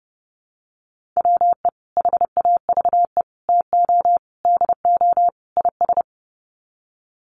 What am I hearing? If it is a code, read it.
Morse code, heard as a keyed tone, about 30 wpm, 700 Hz. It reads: WE HAVE TO DO IS
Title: Sets of 5 Words Encore 30wpm